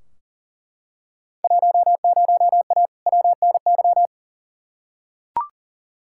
Morse code, heard as a keyed tone, about 40 wpm, 700 Hz. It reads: 10A WNY
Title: Field Day Exchanges - 40wpm